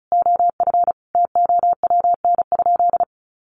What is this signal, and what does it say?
Morse code, heard as a keyed tone, about 34 wpm, 700 Hz. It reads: OF TOWN?
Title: Sentences from Top 1300 Words 35wpm